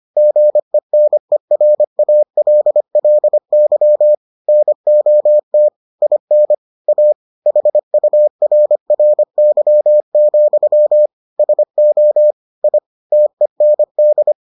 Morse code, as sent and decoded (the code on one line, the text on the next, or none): --. -. .-. .- .-.. .-.. -.-- / -. --- - / .. -. / .- / .... ..- .-. .-. -.-- --..-- / ... --- / .. / - . -. -..
GENERALLY NOT IN A HURRY, SO I TEND